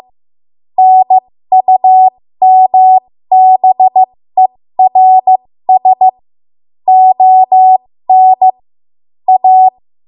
Morse code, heard as a keyed tone, about 15 wpm, spaced as for 13 wpm, 750 Hz. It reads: NUMBERS ON A
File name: Morse / 13WPM